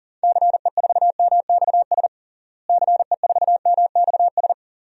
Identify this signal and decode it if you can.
Morse code, heard as a keyed tone, about 37 wpm, 700 Hz. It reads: CE4MXS CE4MXS
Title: DXCC-100 Call Signs - 6 Characters - Repeated 3x - 40wpm